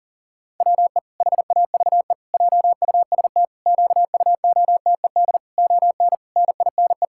Morse code, heard as a keyed tone, about 38 wpm, 700 Hz. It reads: WE HAVE JUST QUOTED ON NINE
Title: Pangram Challenge v2 - 40wpm